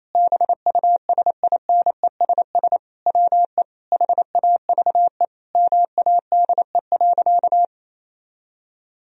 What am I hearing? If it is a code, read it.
Morse code, heard as a keyed tone, about 28 wpm, 700 Hz. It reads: BUSINESS WE HAVE MADE.
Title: Sentences from Top 1000 Words - Review Part 2 - 28wpm